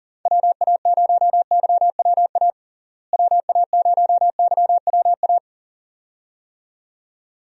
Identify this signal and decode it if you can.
Morse code, heard as a keyed tone, about 40 wpm, 700 Hz. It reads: WA0YWA WA0YWA
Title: USA Call Signs - 6 Characters - Repeated 2x - 40wpm